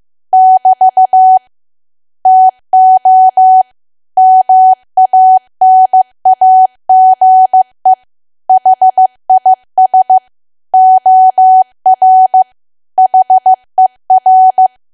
Morse code, heard as a keyed tone, about 15 wpm, 750 Hz. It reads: <BT> TO MANAGE HIS OR HER
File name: cwa / W1AW-Level2 / 15WPM